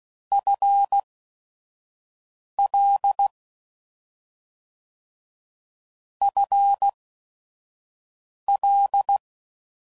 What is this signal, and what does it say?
Morse code, heard as a keyed tone, about 16 wpm, spaced as for 5 wpm, 800 Hz. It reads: FL FL